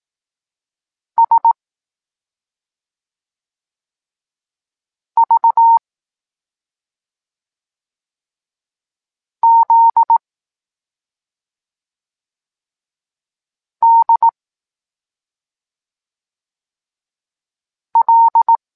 Morse code, heard as a keyed tone, about 18 wpm, 920 Hz. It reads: S V Z D L
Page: Tekst 23.3 – Ciąg dalszy doskonalenia odbioru poznanych liter